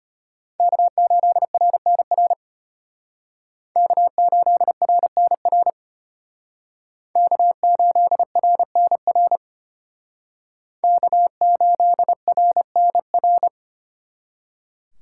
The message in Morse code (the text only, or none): K8RNR K8RNR K8RNR K8RNR